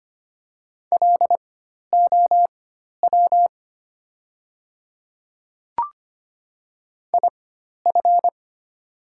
Morse code, heard as a keyed tone, about 25 wpm, spaced as for 12 wpm, 690 Hz. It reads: LOW IF
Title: 4x Character Spacing - Top 200 Words - 25wpm